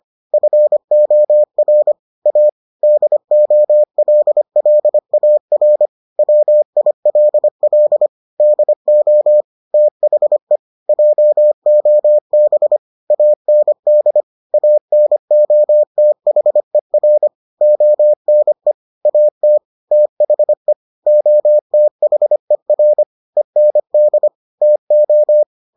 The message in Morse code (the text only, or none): FOR A DOLLAR WILL DO THE JOB AND ANOTHER ONE AT THE OTHER END TO